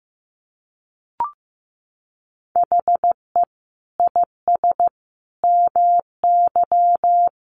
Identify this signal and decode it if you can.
Morse code, heard as a keyed tone, about 15 wpm, 700 Hz. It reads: HE IS MY
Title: Sentences from Top 300 Words 15wpm